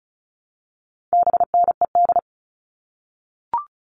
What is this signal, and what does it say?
Morse code, heard as a keyed tone, about 34 wpm, 700 Hz. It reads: BNED
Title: All Cut Numbers - 4-Digit 35wpm